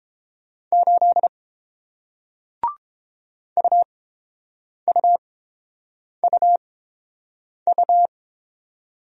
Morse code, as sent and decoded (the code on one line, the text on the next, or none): ---.. / ..- ..- ..- ..-
8 UUUU